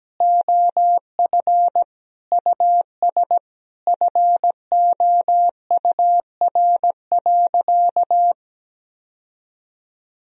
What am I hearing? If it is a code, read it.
Morse code, heard as a keyed tone, about 17 wpm, 700 Hz. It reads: OF US FOUR.
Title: Sentences from Top 1000 Words - Review Part 1 - 17wpm